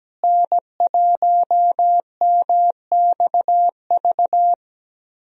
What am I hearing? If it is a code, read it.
Morse code, heard as a keyed tone, about 17 wpm, 700 Hz. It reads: N1MXV
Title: USA Call Signs - 5 Characters - Repeated 3x - 17wpm